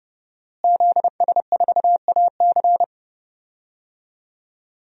Morse code, heard as a keyed tone, about 32 wpm, 700 Hz. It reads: ZS4AC
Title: DXCC-100 Call Signs - 5 Characters - Repeated 2x - 30wpm